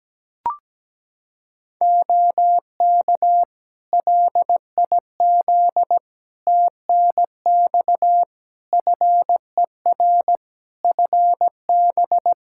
Morse code, heard as a keyed tone, about 17 wpm, 700 Hz. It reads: OK LIZ TNX FER FB
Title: Practice QSO - Exchange 4 - Ending - 17wpm